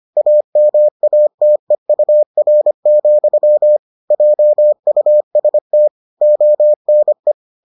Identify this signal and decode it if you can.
Morse code, heard as a keyed tone, about 25 wpm, 600 Hz. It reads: AMATEUR, JUST ONE